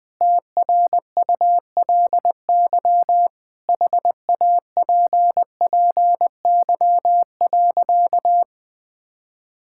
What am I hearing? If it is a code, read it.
Morse code, heard as a keyed tone, about 21 wpm, 700 Hz. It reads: TRULY HAPPY.
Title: Sentences with 7-Word Phrases - 20wpm